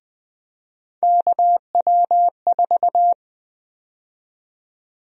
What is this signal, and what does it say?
Morse code, heard as a keyed tone, about 20 wpm, 700 Hz. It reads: KW4
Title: USA Call Sign Prefixes - Repeated 2x - 20wpm